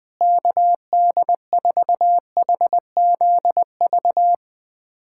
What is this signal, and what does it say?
Morse code, heard as a keyed tone, about 20 wpm, 700 Hz. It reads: KD4HZV
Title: USA Callsign DE Callsign 20wpm